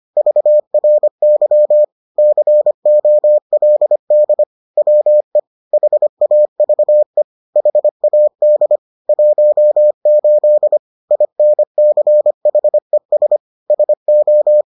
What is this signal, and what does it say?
Morse code, heard as a keyed tone, about 25 wpm, 600 Hz. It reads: VRY COLD WE HAVE HAD 18 INCHES SO
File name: 25wpm-6